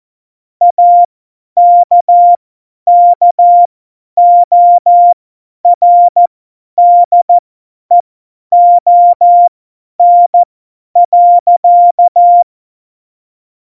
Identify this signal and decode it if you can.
Morse code, heard as a keyed tone, about 14 wpm, spaced as for 10 wpm, 700 Hz. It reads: AKKORDEON.
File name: Morsekurs / Witze / Witz 26. November 2019, 50 F70